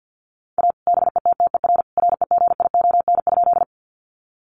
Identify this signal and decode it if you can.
Morse code, heard as a keyed tone, about 50 wpm, 700 Hz. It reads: A BETTER REGION?
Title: Sentences from Top 1500 Words 50wpm